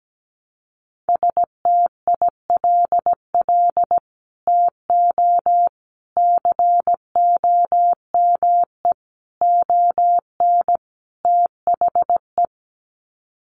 Morse code, heard as a keyed tone, about 17 wpm, 700 Hz. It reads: STILL TO COME ON THE
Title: Sets of 5 Words Encore2 17wpm